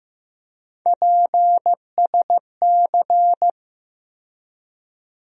Morse code, heard as a keyed tone, about 15 wpm, 700 Hz. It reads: PSC
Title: USA Call Sign Postfixes - 3 Letters 15wpm